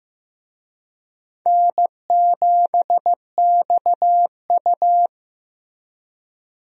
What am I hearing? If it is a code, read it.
Morse code, heard as a keyed tone, about 15 wpm, 700 Hz. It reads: N7XU
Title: USA Call Signs - 4 Characters - Repeated 3x - 15wpm